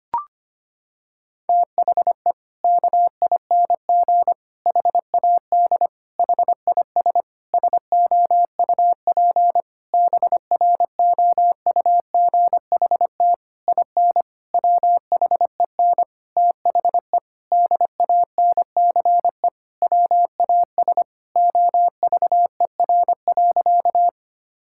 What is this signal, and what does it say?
Morse code, heard as a keyed tone, about 25 wpm, 700 Hz. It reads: THE KING HAD HIS SOUP BROUGHT IN WHEN THE DANCE WAS OVER.
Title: Sentences from Top 1000 Words - Review Part 6 - 25wpm